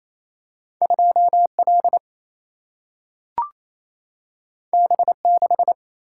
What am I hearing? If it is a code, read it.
Morse code, heard as a keyed tone, about 28 wpm, 700 Hz. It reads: 2 L B 6
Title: Single-Letter Single-Number 28wpm